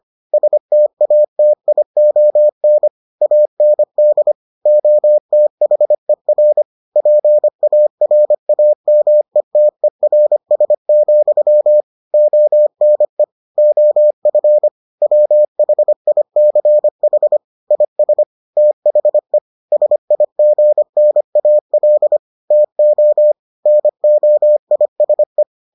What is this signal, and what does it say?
Morse code, heard as a keyed tone, about 25 wpm, 600 Hz. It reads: STATION AND OTHER PARAMETERS, ONE OF WHICH IS THE SIGNAL TO NOISE